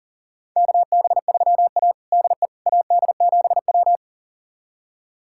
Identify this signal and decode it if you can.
Morse code, heard as a keyed tone, about 40 wpm, 700 Hz. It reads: KB3A DE AD7W
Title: USA Callsign DE Callsign - 4 Characters - 40wpm